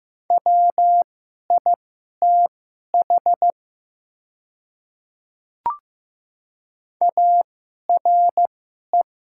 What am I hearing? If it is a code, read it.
Morse code, heard as a keyed tone, about 15 wpm, spaced as for 11 wpm, 700 Hz. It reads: WITH ARE
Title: 2x Character Spacing - Top 100 Words - 15wpm